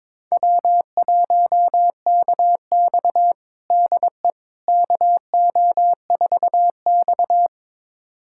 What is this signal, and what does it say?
Morse code, heard as a keyed tone, about 22 wpm, 700 Hz. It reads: W1KX DE KO4X
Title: USA Callsign DE Callsign - 4 Characters - 22wpm